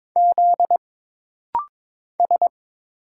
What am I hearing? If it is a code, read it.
Morse code, heard as a keyed tone, about 22 wpm, 700 Hz. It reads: Z S
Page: This rapid-fire timing is geared to push your ICR (Instant Character Recognition) skills.